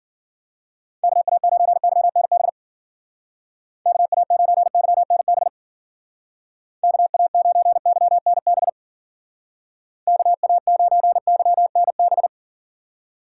Morse code, heard as a keyed tone, about 51 wpm, 700 Hz. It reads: KA9YNB KA9YNB KA9YNB KA9YNB